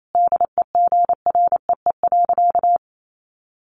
Morse code, heard as a keyed tone, about 28 wpm, 700 Hz. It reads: DEGREE.
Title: Sentences from Top 1500 Words 28wpm